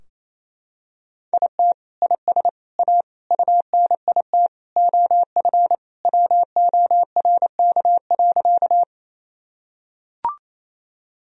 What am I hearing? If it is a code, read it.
Morse code, heard as a keyed tone, about 28 wpm, 700 Hz. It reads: IT IS A UNIT OF WORK.
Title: Sentences from Top 1000 Words - Review Part 2 - 28wpm